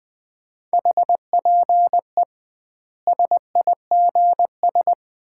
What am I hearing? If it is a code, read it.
Morse code, heard as a keyed tone, about 20 wpm, 700 Hz. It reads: HPE SIGS
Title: Common QSO Phrases - 2x Word-Spacing - 20wpm